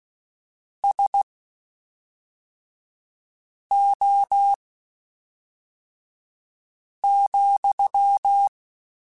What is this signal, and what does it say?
Morse code, heard as a keyed tone, about 16 wpm, spaced as for 3 wpm, 800 Hz. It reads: SO,
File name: (spaced) / QSOs Spaced-mp3s / WD0BGP 1 @5 sp